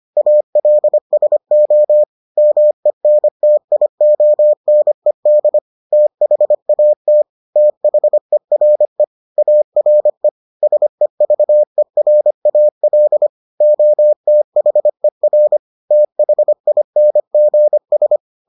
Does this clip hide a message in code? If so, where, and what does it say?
Yes: Morse code from 0.2 to 18.2 s, heard as a keyed tone, about 25 wpm, 600 Hz. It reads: ALSO MENTIONED THAT THERE ARE SEVERAL OTHER THINGS